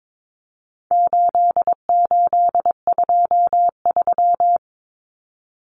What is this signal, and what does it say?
Morse code, heard as a keyed tone, about 22 wpm, 700 Hz. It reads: 8823
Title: Numbers - 4-Digit 22wpm